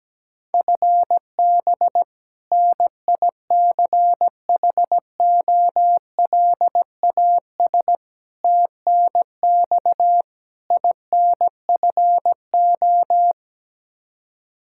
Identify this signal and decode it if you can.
Morse code, heard as a keyed tone, about 17 wpm, 700 Hz. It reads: FB NICHOLAS TNX INFO
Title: K1USN SST QSO Practice 17wpm